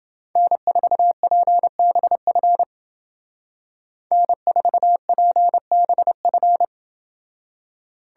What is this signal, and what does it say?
Morse code, heard as a keyed tone, about 29 wpm, 700 Hz. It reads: N4PBF N4PBF